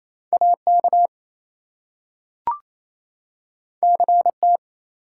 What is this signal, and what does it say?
Morse code, heard as a keyed tone, about 28 wpm, 700 Hz. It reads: AK CT
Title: US States and Canadian Provinces 28wpm